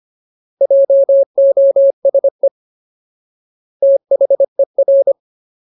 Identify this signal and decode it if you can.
Morse code, heard as a keyed tone, about 25 wpm, 550 Hz. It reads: JOSE THER
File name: Ultimate Word List / 25WPM Ultimate